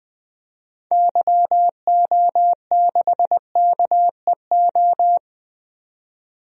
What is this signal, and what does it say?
Morse code, heard as a keyed tone, about 21 wpm, 700 Hz. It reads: YO6KEO
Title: DXCC-100 Call Signs - 6 Characters - 20wpm